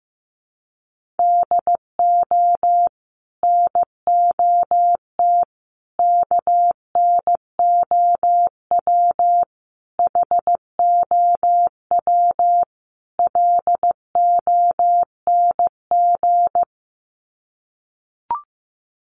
Morse code, heard as a keyed tone, about 15 wpm, 700 Hz. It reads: DO NOT KNOW HOW LONG
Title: Sets of 5 Words Encore2 15wpm